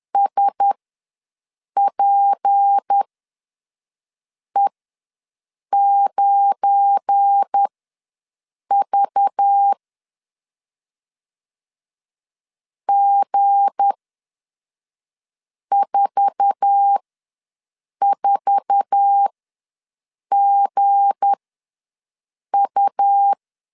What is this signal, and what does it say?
Morse code, heard as a keyed tone, about 10 wpm, spaced as for 5 wpm, 800 Hz. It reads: SPE9V G44GU
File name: LEZIONI / 8 YJ45 / 8 YJ4 ws 10wpm 4020